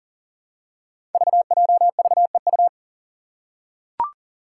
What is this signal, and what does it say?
Morse code, heard as a keyed tone, about 41 wpm, 700 Hz. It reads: VJVEU
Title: Five Challenging Letter-Number Combinations 40wpm